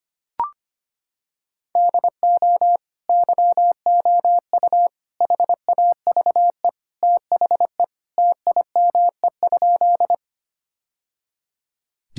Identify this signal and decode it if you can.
Morse code, heard as a keyed tone, about 25 wpm, 700 Hz. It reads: DO YOU HAVE THE TIME?
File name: practice / 25wpm / Sentences with 2-Word Phrases - 25wpm